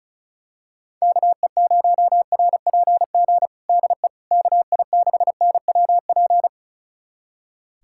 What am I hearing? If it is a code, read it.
Morse code, heard as a keyed tone, about 36 wpm, 700 Hz. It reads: KE0RPG DE KI6NWP
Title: USA Callsign DE Callsign - 6 Characters - 35wpm